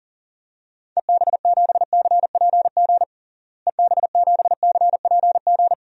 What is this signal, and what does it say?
Morse code, heard as a keyed tone, about 40 wpm, 700 Hz. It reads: EB7CPG EB7CPG
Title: DXCC-100 Call Signs - 6 Characters - Repeated 2x - 40wpm